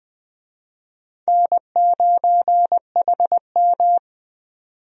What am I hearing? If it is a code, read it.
Morse code, heard as a keyed tone, about 21 wpm, 700 Hz. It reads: N9HM
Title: USA Call Signs - 4 Characters - 20wpm